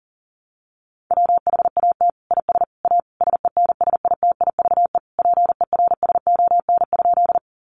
Morse code, heard as a keyed tone, about 40 wpm, 700 Hz. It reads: WHAT IS A SENSITIVE PERSON?
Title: Sentences from Top 1400 Words 40wpm